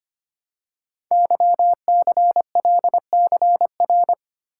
Morse code, heard as a keyed tone, about 25 wpm, 700 Hz. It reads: YCLCR
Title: Code Groups - 5 Letters - 25wpm